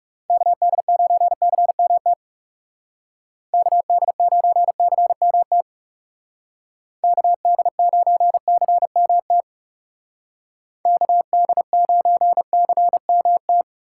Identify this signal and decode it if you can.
Morse code, heard as a keyed tone, about 36 wpm, 700 Hz. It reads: KD9CMT KD9CMT KD9CMT KD9CMT
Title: Speed Racing - USA 2x3 Call Signs 30wpm